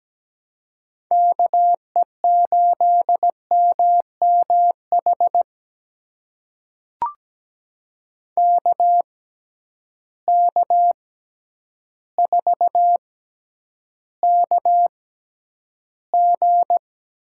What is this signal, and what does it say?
Morse code, heard as a keyed tone, about 17 wpm, spaced as for 9 wpm, 700 Hz. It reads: KE8MMH KK4KG
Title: USA Call Signs - 6 Characters - 6x Character Spacing - 17wpm